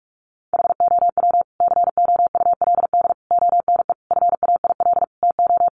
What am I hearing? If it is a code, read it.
Morse code, heard as a keyed tone, about 44 wpm, 700 Hz. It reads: HOW COULD ONE FAIL TO
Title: Sentences from Top 1100 Words 45wpm